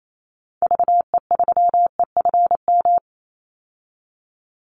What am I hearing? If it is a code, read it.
Morse code, heard as a keyed tone, about 28 wpm, 700 Hz. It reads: VE3EFM
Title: Call Signs - 28wpm